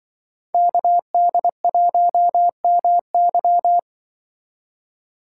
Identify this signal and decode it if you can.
Morse code, heard as a keyed tone, about 25 wpm, 700 Hz. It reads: KD1MY
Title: Speed Racing - USA 2x2 Call Signs 20wpm